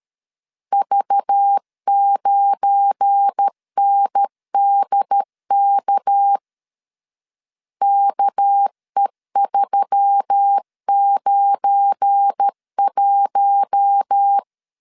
Morse code, heard as a keyed tone, about 13 wpm, 800 Hz. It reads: V9NDK KE391
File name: LEZIONI / 7 GF9Z / 7 GF9 12wpm 2319